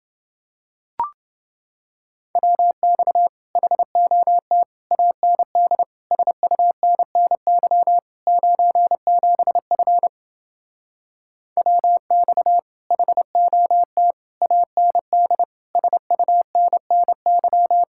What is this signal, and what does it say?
Morse code, heard as a keyed tone, about 30 wpm, 700 Hz. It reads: WX HOT AND SUNNY 97F WX HOT AND SUNNY